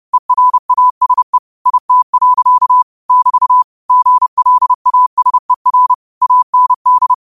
Morse code, heard as a keyed tone, about 30 wpm, 1000 Hz. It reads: ERASE IT. <BT> GLASER AND
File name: qotdfiles / 30wpm